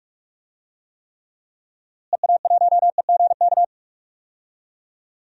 Morse code, heard as a keyed tone, about 42 wpm, 690 Hz. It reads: EA1EGK
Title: DXCC-100 Call Signs - 6 Characters - Repeated 2x - 45wpm